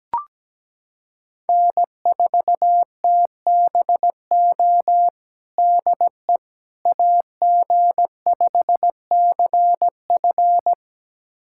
Morse code, heard as a keyed tone, about 17 wpm, 700 Hz. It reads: N4TBO DE AG5CF
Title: USA Callsign DE Callsign - 5 Characters - 17wpm